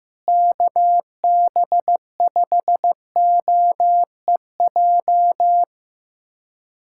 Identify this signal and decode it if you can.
Morse code, heard as a keyed tone, about 16 wpm, 700 Hz. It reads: KB5OEJ
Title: USA Callsign DE Callsign 15wpm